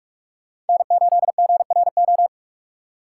Morse code, heard as a keyed tone, about 51 wpm, 700 Hz. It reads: N8GRO
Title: Speed Racing - USA 1x3 Call Signs 45wpm